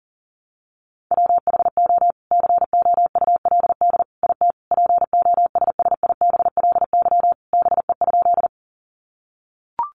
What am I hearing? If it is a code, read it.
Morse code, heard as a keyed tone, about 40 wpm, 700 Hz. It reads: WHO COULD IT POSSIBLY BE?
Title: Sentences from Top 1300 Words 40wpm